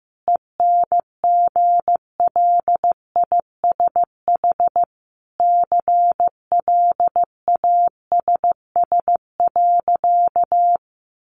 Morse code, heard as a keyed tone, about 15 wpm, 700 Hz. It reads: ENGLISH CLASS.
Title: Additional Easy Phrases 15wpm